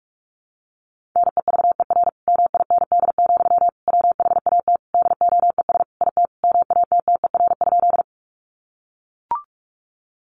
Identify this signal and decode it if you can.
Morse code, heard as a keyed tone, about 44 wpm, 700 Hz. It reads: NEVER MIND, WHAT DOES IT MATTER?
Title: Sentences from Top 700 Words 45wpm